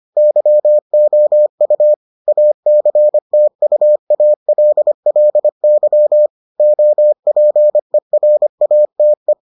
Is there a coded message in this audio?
Yes: Morse code from 0.2 to 9.3 s, heard as a keyed tone, about 25 wpm, 600 Hz. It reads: YOU ACTUALLY OPERATE